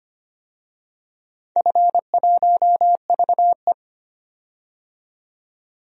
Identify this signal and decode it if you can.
Morse code, heard as a keyed tone, about 25 wpm, 700 Hz. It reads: F1VE
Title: DXCC-100 Call Signs - 4 Characters - 25wpm